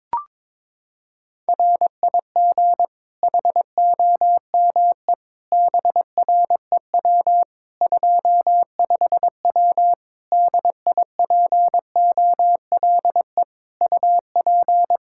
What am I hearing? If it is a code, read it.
Morse code, heard as a keyed tone, about 22 wpm, 700 Hz. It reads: RIG HOME BREW 25W DIPOLE UP
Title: K1USN SST QSO Practice 22wpm